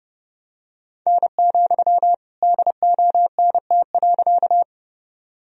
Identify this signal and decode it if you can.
Morse code, heard as a keyed tone, about 30 wpm, 700 Hz. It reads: N, DONT.
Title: Sentences from Top 300 Words and Abbreviations 30wpm